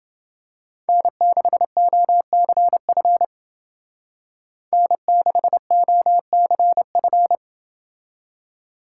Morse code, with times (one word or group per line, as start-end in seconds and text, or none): 0.9-3.2 N6OCF
4.7-7.4 N6OCF